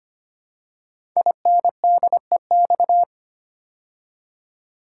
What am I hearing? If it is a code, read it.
Morse code, heard as a keyed tone, about 25 wpm, 700 Hz. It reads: INDEX
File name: 5 Letter Words - First 3 Letters Given - 25wpm